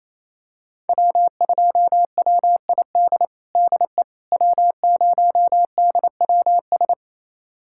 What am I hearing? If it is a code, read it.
Morse code, heard as a keyed tone, about 28 wpm, 700 Hz. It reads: W2WID DE W0DWS
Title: USA Callsign DE Callsign - 5 Characters - 28wpm